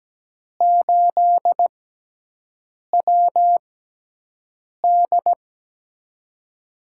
Morse code, heard as a keyed tone, about 17 wpm, spaced as for 6 wpm, 700 Hz. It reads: 8WD
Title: USA Call Signs - 5 Characters - 6x Character Spacing - 17wpm